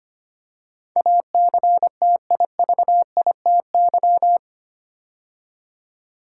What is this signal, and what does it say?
Morse code, heard as a keyed tone, about 25 wpm, 700 Hz. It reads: ACTIVITY
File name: practice / 25wpm / 8 Letter Words - First 3 Letters Given - 25wpm